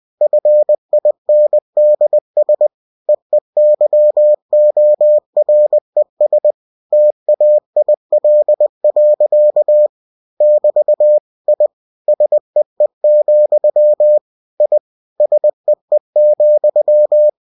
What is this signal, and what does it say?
Morse code, heard as a keyed tone, about 20 wpm, 600 Hz. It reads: FINDS EEYORES TAIL. <BT> I SEE, I SEE,
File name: cw-books / winnie-the-pooh / winnie-20wpm-page-0030